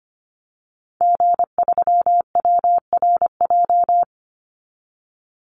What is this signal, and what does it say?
Morse code, heard as a keyed tone, about 25 wpm, 700 Hz. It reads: G3WRJ
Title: Call Signs Repeated 3x - 25wpm